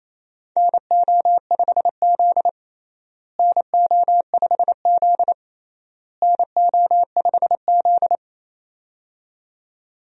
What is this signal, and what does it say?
Morse code, heard as a keyed tone, about 28 wpm, 700 Hz. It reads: NO5Z NO5Z NO5Z
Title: USA Call Signs - 4 Characters - Repeated 3x - 28wpm